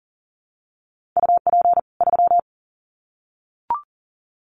Morse code, heard as a keyed tone, about 40 wpm, 700 Hz. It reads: UP 3
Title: Common QSO Phrases 40wpm